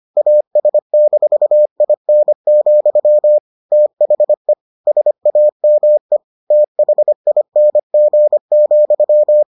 Morse code, heard as a keyed tone, about 25 wpm, 600 Hz. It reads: AS-IN, THE SAME THING,